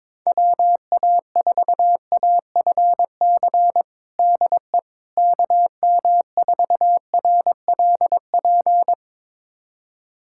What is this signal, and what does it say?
Morse code, heard as a keyed tone, about 22 wpm, 700 Hz. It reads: WA4AFC DE KM4RLP
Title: USA Callsign DE Callsign - 6 Characters - 22wpm